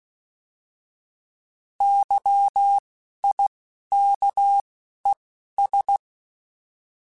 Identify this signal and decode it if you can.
Morse code, heard as a keyed tone, about 16 wpm, 800 Hz. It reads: Y I K E S
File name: Misc / Misc-mp3s / Misc @10